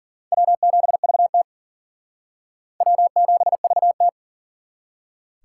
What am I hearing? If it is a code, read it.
Morse code, heard as a keyed tone, about 51 wpm, 700 Hz. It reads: W7VT W7VT